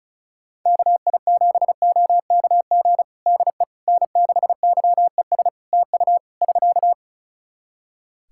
Morse code, heard as a keyed tone, about 36 wpm, 700 Hz. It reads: KI7OKG DE N6YES TU <SK>
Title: Practice QSO - Exchange 4 - Ending - 35wpm